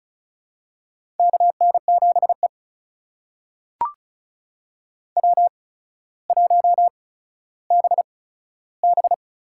Morse code, heard as a keyed tone, about 36 wpm, spaced as for 15 wpm, 700 Hz. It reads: KN7E W1BB
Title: USA Call Signs - 4 Characters - 8x Character Spacing - 35wpm